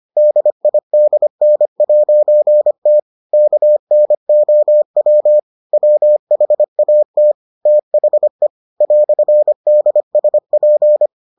Morse code, heard as a keyed tone, about 25 wpm, 600 Hz. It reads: DIDN'T KNOW WHAT THE "DSP